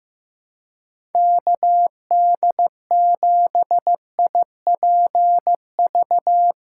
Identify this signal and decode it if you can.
Morse code, heard as a keyed tone, about 15 wpm, 700 Hz. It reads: KD7IPV
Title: USA Callsign DE Callsign 15wpm